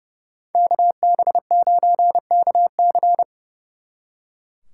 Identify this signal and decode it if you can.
Morse code, heard as a keyed tone, about 37 wpm, 700 Hz. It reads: KB9KC
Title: Speed Racing - USA 2x2 Call Signs 30wpm